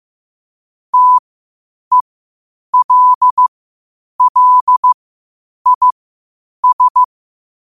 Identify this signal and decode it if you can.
Morse code, heard as a keyed tone, about 15 wpm, spaced as for 8 wpm, 1000 Hz. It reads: TELLIS
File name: 8wpm